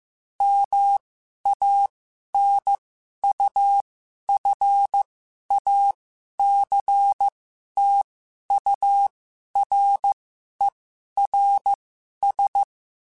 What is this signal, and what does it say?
Morse code, heard as a keyed tone, about 15 wpm, spaced as for 11 wpm, 800 Hz. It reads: MANUFACTURERS